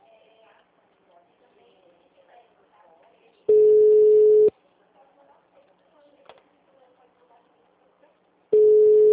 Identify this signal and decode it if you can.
Ringback tone (Europe, 425 Hz).